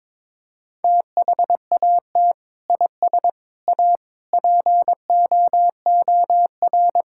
Morse code, heard as a keyed tone, about 22 wpm, 700 Hz. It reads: THAT IS A POOR
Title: Sentences from Top 1000 Words - Review Part 6 - 22wpm